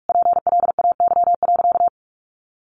Morse code, heard as a keyed tone, about 44 wpm, 700 Hz. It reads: PLAY.
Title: Sentences from Top 1500 Words 45wpm